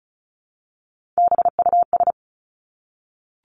Morse code, heard as a keyed tone, about 34 wpm, 700 Hz. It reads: BUS
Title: Top 751-1000 Words 35wpm